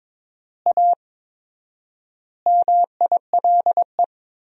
Morse code, heard as a keyed tone, about 22 wpm, 700 Hz. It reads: A MILE
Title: 4x Word-Spacing - Sentences from Top 500 Words - 22wpm